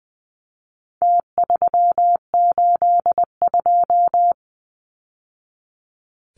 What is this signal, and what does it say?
Morse code, heard as a keyed tone, about 20 wpm, 700 Hz. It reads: T382
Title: Numbers - Contest Style - 4-Digit 20wpm